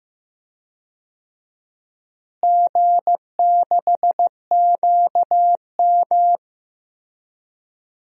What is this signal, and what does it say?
Morse code, heard as a keyed tone, about 15 wpm, 700 Hz. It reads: G6QM
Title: DXCC-100 Call Signs - 4 Characters - Repeated 2x - 15wpm